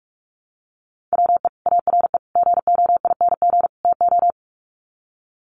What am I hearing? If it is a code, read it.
Morse code, heard as a keyed tone, about 44 wpm, 700 Hz. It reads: WE ARE GOING TO
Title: Sets of 4 Words 45wpm